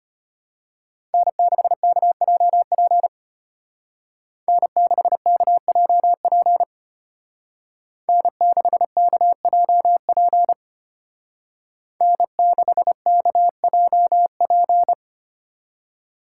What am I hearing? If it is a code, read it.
Morse code, heard as a keyed tone, about 31 wpm, 700 Hz. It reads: N6KJP N6KJP N6KJP N6KJP
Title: Speed Racing - USA 1x3 Call Signs 25wpm